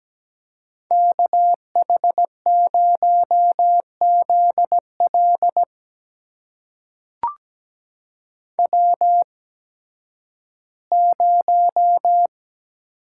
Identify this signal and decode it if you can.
Morse code, heard as a keyed tone, about 17 wpm, spaced as for 11 wpm, 700 Hz. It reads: KH0ZL W0
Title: USA Call Signs - 5 Characters - 8x Character Spacing - 17wpm